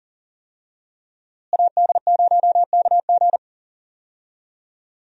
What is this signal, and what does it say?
Morse code, heard as a keyed tone, about 40 wpm, 700 Hz. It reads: AD0KG
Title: USA Call Signs - 5 Characters - Repeated 3x - 40wpm